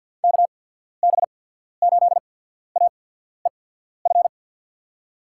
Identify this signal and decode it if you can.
Morse code, heard as a keyed tone, about 47 wpm, spaced as for 14 wpm, 700 Hz. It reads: KB8AEF
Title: USA Call Signs - 6 Characters - 8x Character Spacing - 50wpm